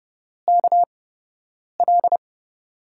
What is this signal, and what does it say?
Morse code, heard as a keyed tone, about 30 wpm, spaced as for 8 wpm, 700 Hz. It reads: KL
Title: USA Call Signs - 8x Character Spacing - 30wpm